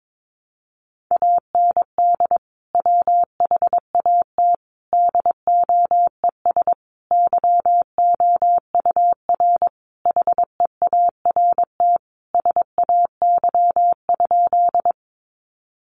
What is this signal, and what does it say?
Morse code, heard as a keyed tone, about 22 wpm, 700 Hz. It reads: AND WHAT DOES YOUR HEART SAY?
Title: Sentences from Top 1200 Words 22wpm